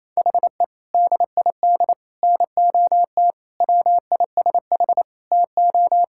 Morse code, heard as a keyed tone, about 27 wpm, 700 Hz. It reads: HE DID NOT WISH TO
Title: Sentences from Top 1600 Words 28wpm